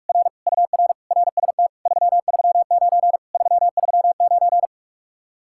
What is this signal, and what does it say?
Morse code, heard as a keyed tone, about 47 wpm, 700 Hz. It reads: R UR RST 339 339